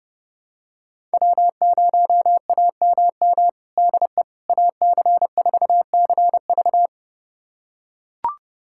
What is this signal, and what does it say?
Morse code, heard as a keyed tone, about 30 wpm, 700 Hz. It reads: W0AMM DE AC4CV
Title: USA Callsign DE Callsign 30wpm